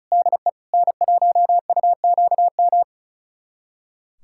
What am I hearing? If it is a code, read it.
Morse code, heard as a keyed tone, about 34 wpm, 700 Hz. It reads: DE N1UQM
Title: USA Callsign DE Callsign - 5 Characters - 35wpm